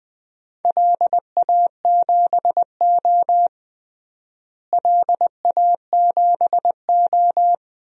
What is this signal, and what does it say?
Morse code, heard as a keyed tone, about 20 wpm, 700 Hz. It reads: LA7O LA7O
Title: DXCC-100 Call Signs - 4 Characters - Repeated 3x - 20wpm